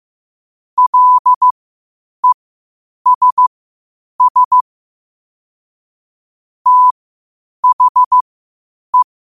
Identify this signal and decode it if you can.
Morse code, heard as a keyed tone, about 15 wpm, spaced as for 8 wpm, 1000 Hz. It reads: LESS THE
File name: qotdfiles / 8wpm